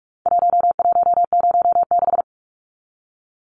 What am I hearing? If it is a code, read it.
Morse code, heard as a keyed tone, about 44 wpm, 700 Hz. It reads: 1106
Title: Numbers - Contest Style - 4-Digit 45wpm